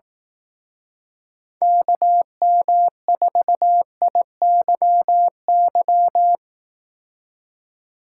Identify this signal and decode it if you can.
Morse code, heard as a keyed tone, about 18 wpm, 700 Hz. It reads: KM4IYY
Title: Speed Racing - USA Call Signs 15wpm